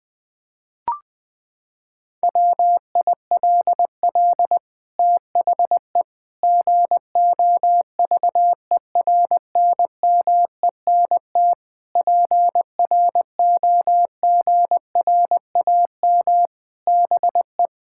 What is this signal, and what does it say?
Morse code, heard as a keyed tone, about 21 wpm, 700 Hz. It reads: WILL THE GOVERNMENT PROGRAM BE
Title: Sentences from Top 1000 Words - Review Part 5 - 20wpm